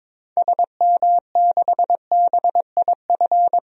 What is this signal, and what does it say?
Morse code, heard as a keyed tone, about 22 wpm, 700 Hz. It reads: SM6BIF
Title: DXCC-100 Call Signs - 6 Characters - 22wpm